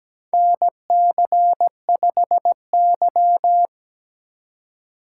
Morse code, heard as a keyed tone, about 17 wpm, 700 Hz. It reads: NC5Y
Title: USA Call Signs - 4 Characters - 8x Character Spacing - 17wpm